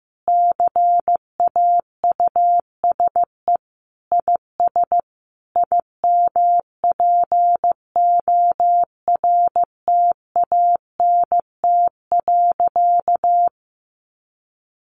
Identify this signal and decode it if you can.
Morse code, heard as a keyed tone, about 15 wpm, 700 Hz. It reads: CAUSE IS IMPORTANT.
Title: Sentences from Top 1200 Words 15wpm